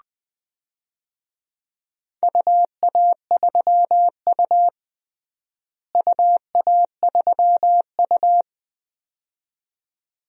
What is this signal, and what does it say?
Morse code, heard as a keyed tone, about 20 wpm, 700 Hz. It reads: UA3U UA3U
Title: DXCC-100 Call Signs - 4 Characters - Repeated 2x - 20wpm